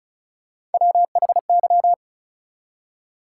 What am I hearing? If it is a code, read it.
Morse code, heard as a keyed tone, about 34 wpm, 700 Hz. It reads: WHY
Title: Three Letter Words - 35wpm